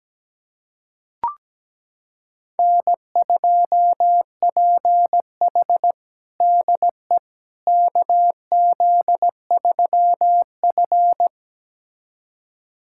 Morse code, heard as a keyed tone, about 17 wpm, 700 Hz. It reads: N2PH DE KZ3F
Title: USA Callsign DE Callsign - 4 Characters - 17wpm